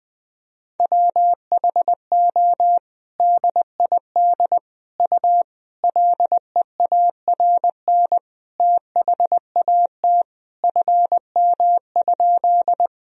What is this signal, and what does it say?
Morse code, heard as a keyed tone, about 20 wpm, 700 Hz. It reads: WHO DID U LEARN THAT FM?
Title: Sentences from Top 300 Words and Abbreviations 20wpm